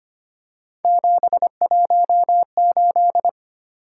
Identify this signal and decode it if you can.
Morse code, heard as a keyed tone, about 25 wpm, 700 Hz. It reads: 718
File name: Three Letter-Number Combinations - 25wpm